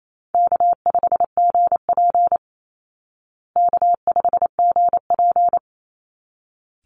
Morse code, heard as a keyed tone, about 28 wpm, 700 Hz. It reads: K5GP K5GP
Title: Easy Call Signs Repeated 2x - 28wpm